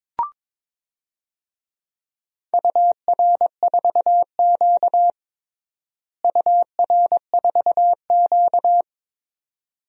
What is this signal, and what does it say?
Morse code, heard as a keyed tone, about 22 wpm, 700 Hz. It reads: UR4Q UR4Q
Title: DXCC-100 Call Signs - 4 Characters - Repeated 2x - 22wpm